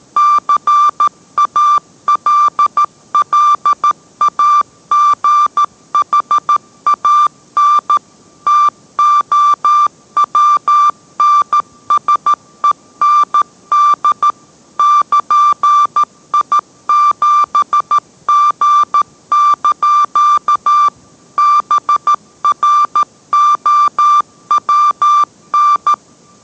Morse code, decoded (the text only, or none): CALLAGHAN TOWNSEND <KN>I7G) BROWN